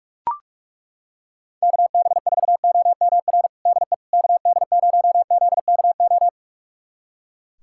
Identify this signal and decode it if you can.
Morse code, heard as a keyed tone, about 41 wpm, 700 Hz. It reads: KB4OMR DE KD0ZKO
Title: USA Callsign DE Callsign - 6 Characters - 45wpm